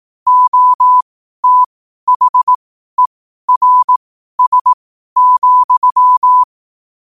Morse code, heard as a keyed tone, about 18 wpm, spaced as for 12 wpm, 1000 Hz. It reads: OTHERS,
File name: qotdfiles / 12wpm